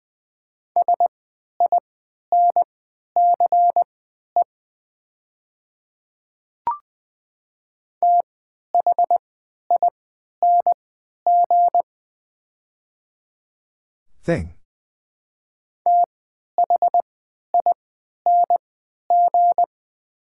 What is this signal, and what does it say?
Morse code, heard as a keyed tone, about 20 wpm, spaced as for 11 wpm, 700 Hz. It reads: SINCE THING THING
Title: 3x Character Spacing - Top 300 Words - 20wpm